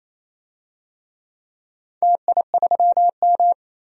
Morse code, heard as a keyed tone, about 28 wpm, 700 Hz. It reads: TI3M
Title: DXCC-100 Call Signs - 4 Characters - 28wpm